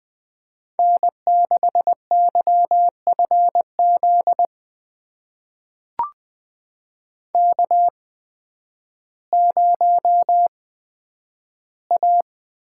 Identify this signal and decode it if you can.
Morse code, heard as a keyed tone, about 20 wpm, spaced as for 11 wpm, 700 Hz. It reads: N6YFZ K0A
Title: USA Call Signs - 5 Characters - 8x Character Spacing - 20wpm